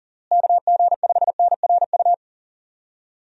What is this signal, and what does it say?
Morse code, heard as a keyed tone, about 37 wpm, 700 Hz. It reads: KG5NRU
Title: USA Call Signs - 6 Characters - Repeated 3x - 40wpm